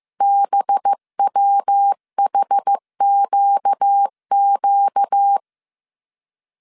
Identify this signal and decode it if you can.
Morse code, heard as a keyed tone, about 15 wpm, 800 Hz. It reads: BWHQQ